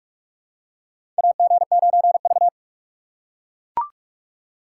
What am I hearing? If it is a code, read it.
Morse code, heard as a keyed tone, about 44 wpm, 700 Hz. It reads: AG9V
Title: USA Call Signs - 4 Characters - 2x Character Spacing - 45wpm